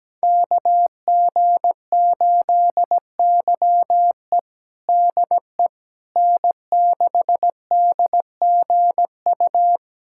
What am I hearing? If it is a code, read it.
Morse code, heard as a keyed tone, about 17 wpm, 700 Hz. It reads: KG8YE DE N6DGU
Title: USA Callsign DE Callsign - 5 Characters - 17wpm